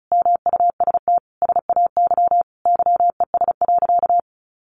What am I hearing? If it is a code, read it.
Morse code, heard as a keyed tone, about 34 wpm, 700 Hz. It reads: MUST SAY YES.
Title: Sentences from Top 500 Words 35wpm